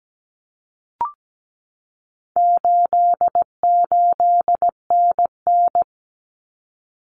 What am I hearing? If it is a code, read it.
Morse code, heard as a keyed tone, about 17 wpm, 700 Hz. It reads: 88NN
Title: Common Cut Numbers - 4-Digit 17wpm